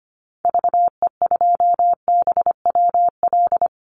Morse code, heard as a keyed tone, about 25 wpm, 700 Hz. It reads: VE2BWL
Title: Call Signs - 25wpm